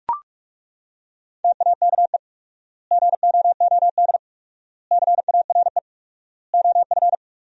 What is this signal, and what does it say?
Morse code, heard as a keyed tone, about 41 wpm, 700 Hz. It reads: TAKE GOOD CARE OF
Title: 4x Word-Spacing - Sentences from Top 400 Words - 45wpm